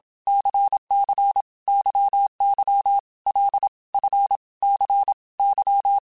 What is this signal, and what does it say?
Morse code, heard as a keyed tone, about 27 wpm, 800 Hz. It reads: CC YY L F C Y
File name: Basic Letter Groups By7Groups / LFCYQ / LFCYQ@25